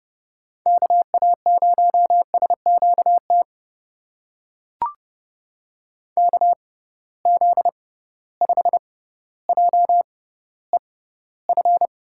Morse code, heard as a keyed tone, about 30 wpm, spaced as for 15 wpm, 700 Hz. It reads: KA0SQT KZ5JEF
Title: USA Call Signs - 6 Characters - 6x Character Spacing - 30wpm